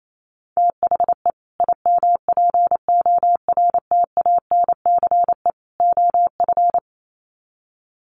Morse code, heard as a keyed tone, about 28 wpm, 700 Hz. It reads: THE IMPORTANCE OF
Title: Sets of 3 Words 28wpm